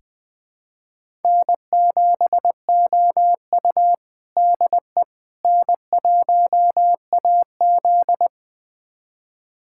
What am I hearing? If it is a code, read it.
Morse code, heard as a keyed tone, about 20 wpm, 700 Hz. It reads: N7OU DE N1AZ
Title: USA Callsign DE Callsign - 4 Characters - 20wpm